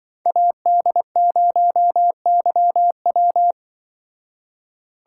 Morse code, heard as a keyed tone, about 24 wpm, 700 Hz. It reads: AD0YW
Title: Speed Racing - USA 2x2 Call Signs 20wpm